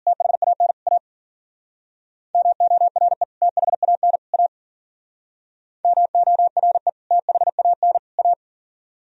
Morse code, heard as a keyed tone, about 46 wpm, 700 Hz. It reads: THAN A MORE THAN A MORE THAN A
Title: Speed-Racing - Sets of 3 Words 40wpm